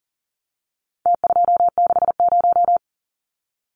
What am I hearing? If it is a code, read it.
Morse code, heard as a keyed tone, about 40 wpm, 700 Hz. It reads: T260
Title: Numbers - Contest Style - 4-Digit 40wpm